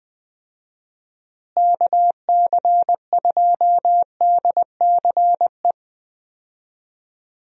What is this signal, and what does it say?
Morse code, heard as a keyed tone, about 21 wpm, 700 Hz. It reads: KC2DCE
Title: USA Call Signs - 6 Characters - Repeated 3x - 20wpm